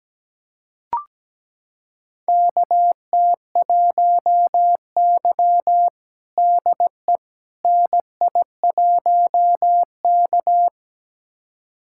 Morse code, heard as a keyed tone, about 17 wpm, 700 Hz. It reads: KT1Y DE NI1K
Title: USA Callsign DE Callsign - 4 Characters - 17wpm